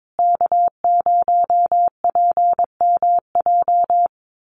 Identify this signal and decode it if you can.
Morse code, heard as a keyed tone, about 22 wpm, 700 Hz. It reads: K0PMJ
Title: USA Call Signs Repeated 3x 22wpm